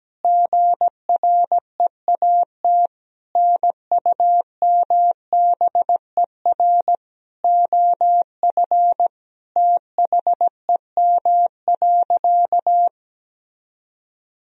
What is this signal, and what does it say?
Morse code, heard as a keyed tone, about 17 wpm, 700 Hz. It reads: GREAT NUMBER OF THEM.
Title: Sentences from Top 1000 Words - Review Part 1 - 17wpm